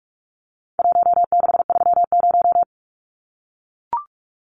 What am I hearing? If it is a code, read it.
Morse code, heard as a keyed tone, about 44 wpm, 700 Hz. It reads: 1630
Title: Numbers - Contest Style - 4-Digit 45wpm